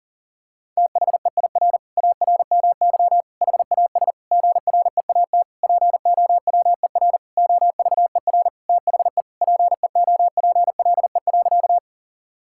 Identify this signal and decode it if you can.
Morse code, heard as a keyed tone, about 40 wpm, 700 Hz. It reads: THEIR ARMY HAS GREAT POWER OVER THE PEOPLE.
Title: Sentences from Top 1000 Words - Review Part 5 - 40wpm